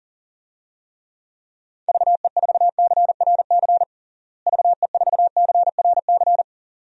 Morse code, heard as a keyed tone, about 40 wpm, 700 Hz. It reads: VE4CRC VE4CRC
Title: DXCC-100 Call Signs - 6 Characters - Repeated 2x - 40wpm